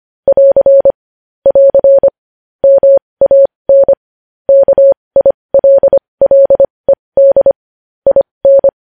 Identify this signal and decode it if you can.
Morse code, heard as a keyed tone, about 25 wpm, spaced as for 20 wpm, 550 Hz. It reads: <AR> <AR> MAN KILLED IN
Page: news headlines in morse code at 20 wpm